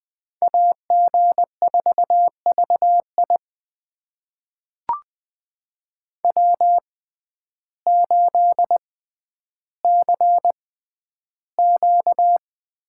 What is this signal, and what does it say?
Morse code, heard as a keyed tone, about 20 wpm, spaced as for 11 wpm, 700 Hz. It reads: AG4VI W8CQ
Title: USA Call Signs - 5 Characters - 6x Character Spacing - 20wpm